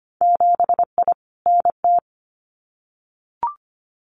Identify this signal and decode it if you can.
Morse code, heard as a keyed tone, about 25 wpm, 700 Hz. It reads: 7I NT
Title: Winter Field Day Exchanges Encore 25wpm